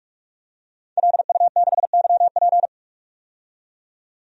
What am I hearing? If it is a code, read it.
Morse code, heard as a keyed tone, about 43 wpm, 700 Hz. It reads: LU6YP
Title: DXCC-100 Call Signs - 5 Characters - 45wpm